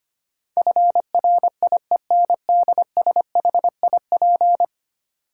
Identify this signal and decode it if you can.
Morse code, heard as a keyed tone, about 25 wpm, 700 Hz. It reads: FRIENDSHIP
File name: 10 Letter Words - First 3 Letters Given - 25wpm